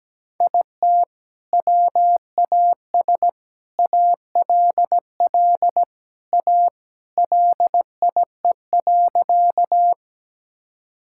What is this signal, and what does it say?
Morse code, heard as a keyed tone, about 17 wpm, 700 Hz. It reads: IT WAS ALL A LIE.
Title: Sentences from Top 1600 Words 17wpm